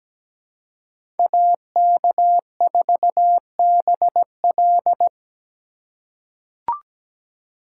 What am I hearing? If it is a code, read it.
Morse code, heard as a keyed tone, about 17 wpm, 700 Hz. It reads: AK4BL
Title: USA Call Signs - 5 Characters - 3x Character Spacing - 17wpm